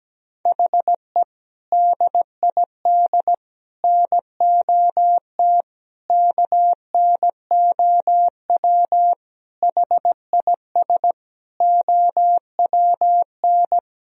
Morse code, heard as a keyed tone, about 17 wpm, 700 Hz. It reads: HE DID NOT KNOW HIS OWN
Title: Sentences from Top 1000 Words - Review Part 2 - 17wpm